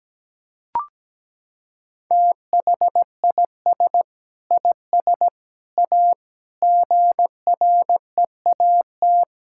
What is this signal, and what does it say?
Morse code, heard as a keyed tone, about 17 wpm, 700 Hz. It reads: THIS IS A GREAT
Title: Sentences from Top 1000 Words - Review Part 6 - 17wpm